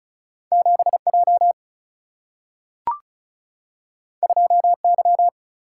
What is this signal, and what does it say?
Morse code, heard as a keyed tone, about 36 wpm, 700 Hz. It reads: 7 J 2 Y
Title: Single-Letter Single-Number 35wpm